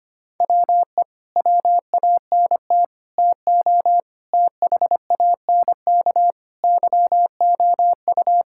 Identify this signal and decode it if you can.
Morse code, heard as a keyed tone, about 25 wpm, 700 Hz. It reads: WE WANT TO THANK YOU
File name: Sets of 9 Words - 25wpm